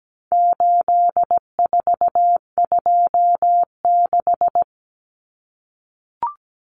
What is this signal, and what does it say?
Morse code, heard as a keyed tone, about 17 wpm, 700 Hz. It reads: 8426
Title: Numbers - 4-Digit 17wpm